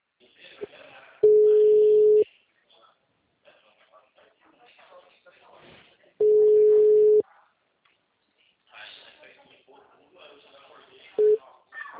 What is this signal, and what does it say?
Ringback tone (Europe, 425 Hz), with speech.